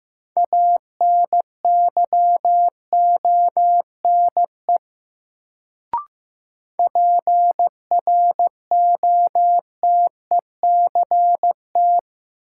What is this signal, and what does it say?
Morse code, heard as a keyed tone, about 15 wpm, 700 Hz. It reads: ANYONE PROTECT
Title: Sentences from Top 900 Words - New Word Drill - 15wpm